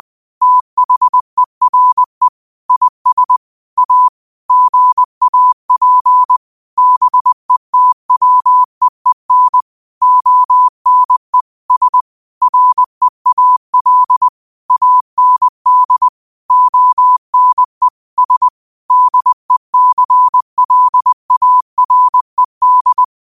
Morse code, heard as a keyed tone, about 20 wpm, 1000 Hz. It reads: THERE IS A GAP BETWEEN ONE S REAL AND ONE S DECLARED
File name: qotdfiles / 20wpm